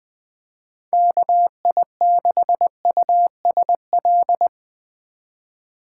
Morse code, heard as a keyed tone, about 22 wpm, 700 Hz. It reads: KI6USL
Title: Speed Racing - USA 2x3 Call Signs 20wpm